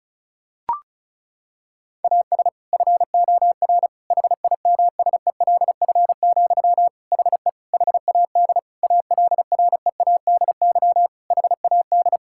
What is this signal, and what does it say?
Morse code, heard as a keyed tone, about 36 wpm, 700 Hz. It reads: AS FOR HIMSELF, HE HAD ALREADY HAD
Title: Sentences from Top 1600 Words 35wpm